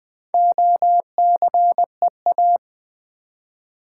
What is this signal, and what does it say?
Morse code, heard as a keyed tone, about 20 wpm, 700 Hz. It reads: OCEA
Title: Build A Word - 1 Letter At A Time - 5-Letter Words - 20wpm